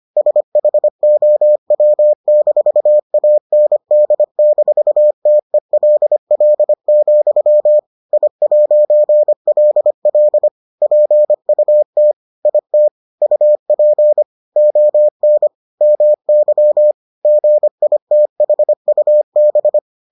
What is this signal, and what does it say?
Morse code, heard as a keyed tone, about 25 wpm, 600 Hz. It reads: SHOW-AND-TELL, I'LL PUT IT UP ON MY GITHUB